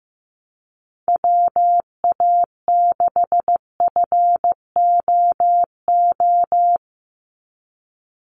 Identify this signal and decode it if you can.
Morse code, heard as a keyed tone, about 15 wpm, 700 Hz. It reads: WA6FOO
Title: USA Call Signs 15wpm